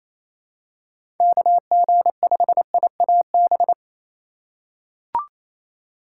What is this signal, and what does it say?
Morse code, heard as a keyed tone, about 28 wpm, 700 Hz. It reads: KG5IAB
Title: USA Call Signs - 6 Characters - 2x Character Spacing - 28wpm